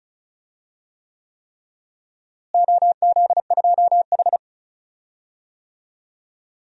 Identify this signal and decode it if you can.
Morse code, heard as a keyed tone, about 36 wpm, 700 Hz. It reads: OZ2H
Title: DXCC-100 Call Signs - 4 Characters - Repeated 3x - 35wpm